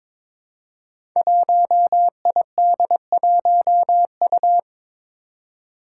Morse code, heard as a keyed tone, about 21 wpm, 700 Hz. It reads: 1ID1U
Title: Five Challenging Letter-Number Combinations 22wpm